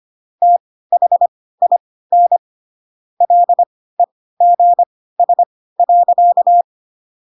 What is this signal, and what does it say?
Morse code, heard as a keyed tone, about 25 wpm, spaced as for 16 wpm, 700 Hz. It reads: THIN LEGS.
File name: Pinocchio / Pinocchio ch2 10 25 ik1hge